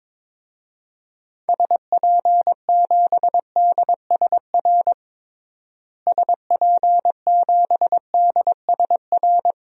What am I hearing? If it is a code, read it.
Morse code, heard as a keyed tone, about 22 wpm, 700 Hz. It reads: SP7DSR SP7DSR
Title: DXCC-100 Call Signs - 6 Characters - Repeated 3x - 22wpm